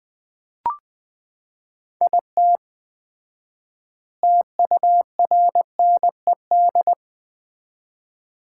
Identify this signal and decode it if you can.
Morse code, heard as a keyed tone, about 20 wpm, 700 Hz. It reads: IT TURNED
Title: 4x Word-Spacing - Sets of 5 Words - 20wpm